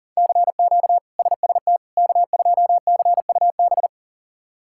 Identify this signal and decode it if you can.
Morse code, heard as a keyed tone, about 40 wpm, 700 Hz. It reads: CQ SST K2CUB
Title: K1USN SST QSO Practice - 40wpm